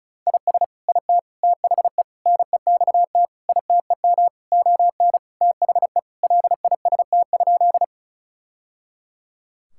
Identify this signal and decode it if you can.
Morse code, heard as a keyed tone, about 34 wpm, 700 Hz. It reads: IS IT THE NEXT ITEM ON THE LIST?
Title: Sentences from Top 1000 Words - Review Part 6 - 35wpm